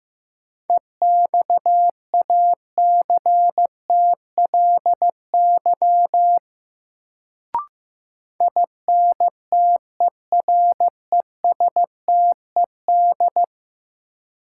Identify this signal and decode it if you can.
Morse code, heard as a keyed tone, about 15 wpm, 700 Hz. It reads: EXACTLY INTERESTED
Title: Sentences from Top 900 Words - New Word Drill - 15wpm